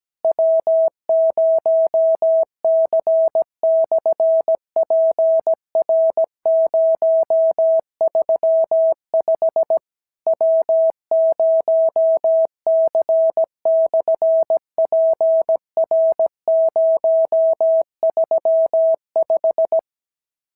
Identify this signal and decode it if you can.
Morse code, heard as a keyed tone, about 17 wpm, 650 Hz. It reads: W0C/PR035 W0C/PR035
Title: SOTA QSO Practice 17wpm